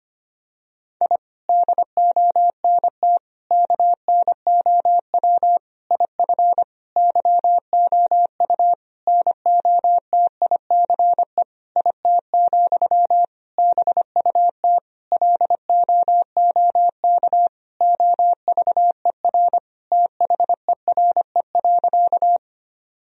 Morse code, heard as a keyed tone, about 25 wpm, 700 Hz. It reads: I DONT KNOW IF YOU NOTICE IT, BUT LOOK OVER THERE.
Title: Sentences from Top 1000 Words - Review Part 2 - 25wpm